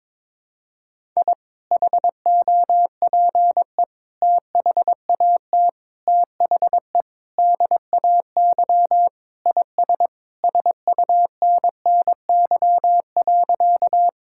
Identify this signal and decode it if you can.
Morse code, heard as a keyed tone, about 21 wpm, 700 Hz. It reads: I HOPE THAT THE DAY IS SUNNY.
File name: practice / 22wpm / Sentences with 2-Word Phrases - 22wpm